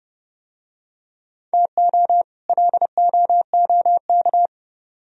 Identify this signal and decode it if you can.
Morse code, heard as a keyed tone, about 29 wpm, 700 Hz. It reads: TO LOOK